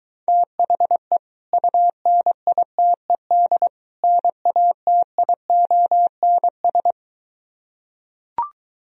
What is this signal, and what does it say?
Morse code, heard as a keyed tone, about 23 wpm, 700 Hz. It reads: THE UNITED NATIONS